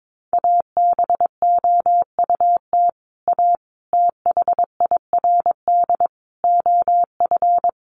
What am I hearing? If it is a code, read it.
Morse code, heard as a keyed tone, about 22 wpm, 700 Hz. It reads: ABOUT A THIRD OF
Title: Sets of 5 Words Encore2 22wpm